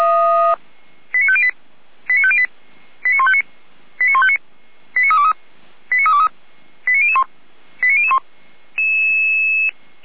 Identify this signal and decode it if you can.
Five-tone selective call (ZVEI), heard as quick runs of five tones, tones of about 70 ms. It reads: 89508 89508 89159 89159 89323 89323 89001 89001